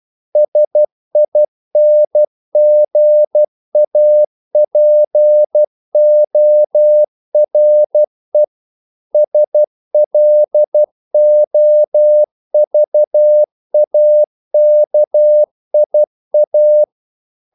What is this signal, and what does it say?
Morse code, heard as a keyed tone, about 12 wpm, 600 Hz. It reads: SINGAPORE SLOVAKIA
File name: CW / Practice / NationalCapitals12